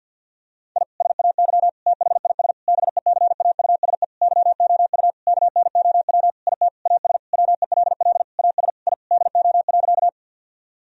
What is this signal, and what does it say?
Morse code, heard as a keyed tone, about 50 wpm, 700 Hz. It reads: I SAY THIS BECAUSE YOU KNOW IT AS WELL AS I DO.
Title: Sentences from Top 1000 Words - Review Part 3 - 50wpm